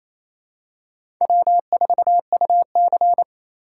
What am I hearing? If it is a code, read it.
Morse code, heard as a keyed tone, about 28 wpm, 700 Hz. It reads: W4UC
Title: USA Call Signs - 4x Character Spacing - 28wpm